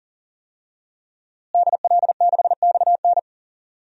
Morse code, heard as a keyed tone, about 40 wpm, 700 Hz. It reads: DL6XN
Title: DXCC-100 Call Signs - 5 Characters - Repeated 2x - 40wpm